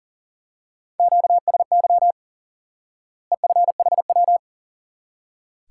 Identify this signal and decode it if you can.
Morse code, heard as a keyed tone, about 40 wpm, 700 Hz. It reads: QSY EFHW
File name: IQE Course 3 - Lesson 89 - YAGI - Two Elements - 40wpm